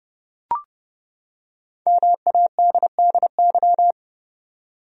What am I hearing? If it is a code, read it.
Morse code, heard as a keyed tone, about 30 wpm, 700 Hz. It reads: MADDY
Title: USA Common Names - Max 5 Letters - 30wpm